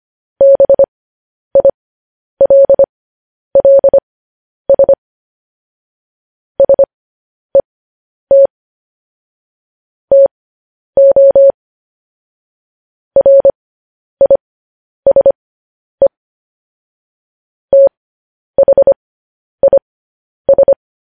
Morse code, heard as a keyed tone, about 25 wpm, spaced as for 10 wpm, 550 Hz. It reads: BILLS SET TO RISE THIS
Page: news-headlines-in-morse-code-at-10-wpm